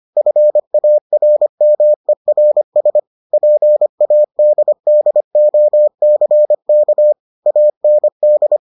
Morse code, heard as a keyed tone, about 25 wpm, 600 Hz. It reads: FARMERS PADDOCK AND